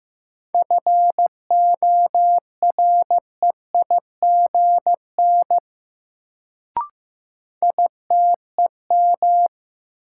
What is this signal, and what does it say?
Morse code, heard as a keyed tone, about 15 wpm, 700 Hz. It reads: FOREIGN ITEM
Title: Sentences from Top 1400 Words - New Word Drill Alt - 15wpm